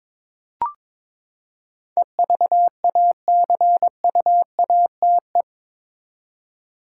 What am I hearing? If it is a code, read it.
Morse code, heard as a keyed tone, about 22 wpm, 700 Hz. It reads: EVACUATE
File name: practice / 22wpm / Eight Letter Words - 22wpm